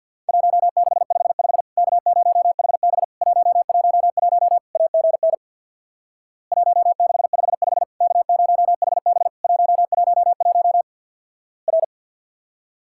Morse code, heard as a keyed tone, about 50 wpm, 700 Hz. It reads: 1655 K0HB 111 AGN 1655 K0HB 111 R
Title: Worked All Europe DX Contest - QTC Exchange with Repeats - 50wpm